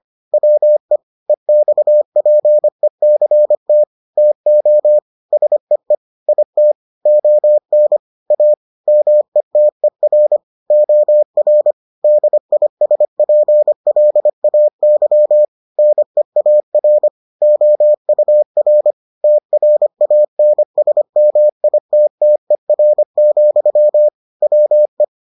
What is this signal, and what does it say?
Morse code, heard as a keyed tone, about 25 wpm, 600 Hz. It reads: WE EXPECT TO SEE IT ON A METER OR DISPLAY NEAR OUR TRANSMITTER, WE